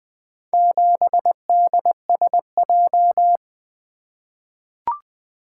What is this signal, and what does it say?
Morse code, heard as a keyed tone, about 20 wpm, 700 Hz. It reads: 7DSJ
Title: Four Letter-Number Combinations - 20wpm